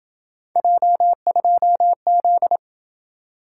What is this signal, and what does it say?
Morse code, heard as a keyed tone, about 26 wpm, 700 Hz. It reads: J2Z